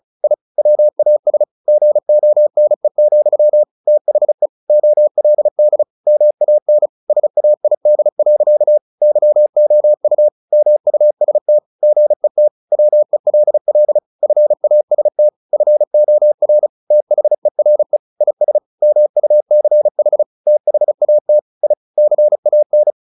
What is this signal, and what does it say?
Morse code, heard as a keyed tone, about 34 wpm, 600 Hz. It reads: I WAS GONE, THE OLD MAN SAID. YOU MUST GET WELL FAST FOR THERE IS MUCH THAT I CAN
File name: the old man and the sea 2 35wpm